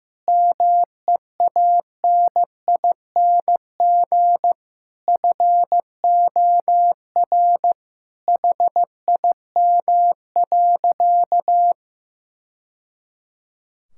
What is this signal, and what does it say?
Morse code, heard as a keyed tone, about 15 wpm, 700 Hz. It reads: MEANING FOR HIM.
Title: Sentences from Top 1000 Words - Review Part 6 - 15wpm